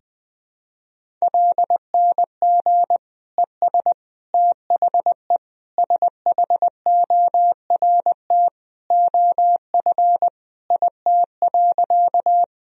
Morse code, heard as a keyed tone, about 20 wpm, 700 Hz. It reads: LNG ES THE SHORT OF IT.
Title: Sentences from Top 400 Words and Abbreviations 20wpm